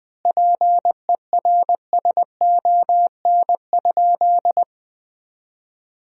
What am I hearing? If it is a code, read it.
Morse code, heard as a keyed tone, about 20 wpm, 700 Hz. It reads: PERSON?
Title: 4x Word-Spacing - Sentences from Top 400 Words - 20wpm